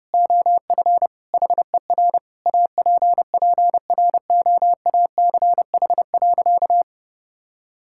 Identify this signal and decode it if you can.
Morse code, heard as a keyed tone, about 30 wpm, 700 Hz. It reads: OF HER APPROACH.
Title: Sentences from Top 1000 Words - Review Part 6 - 30wpm